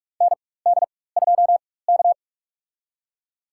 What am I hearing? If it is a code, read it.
Morse code, heard as a keyed tone, about 46 wpm, spaced as for 21 wpm, 700 Hz. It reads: ND2K
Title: USA Call Signs - 4 Characters - 4x Character Spacing - 45wpm